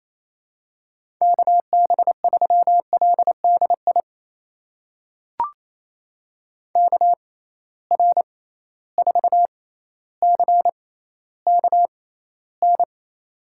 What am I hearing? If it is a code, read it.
Morse code, heard as a keyed tone, about 28 wpm, spaced as for 14 wpm, 700 Hz. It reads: KB3LDI KR4CKN
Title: USA Call Signs - 6 Characters - 6x Character Spacing - 28wpm